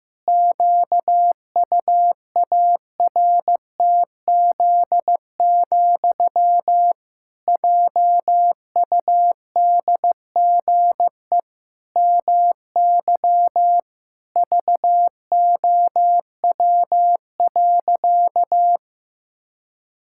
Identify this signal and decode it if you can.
Morse code, heard as a keyed tone, about 15 wpm, 700 Hz. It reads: QUARTZ, JUDGE MY VOW.
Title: Pangram Challenge v2 - 15wpm